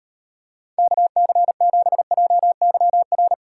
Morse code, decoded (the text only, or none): KC7JYR